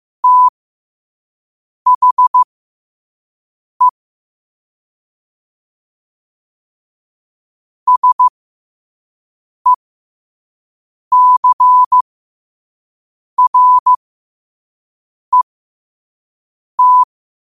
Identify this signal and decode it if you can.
Morse code, heard as a keyed tone, about 15 wpm, spaced as for 5 wpm, 1000 Hz. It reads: THE SECRET
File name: qotdfiles / 5wpm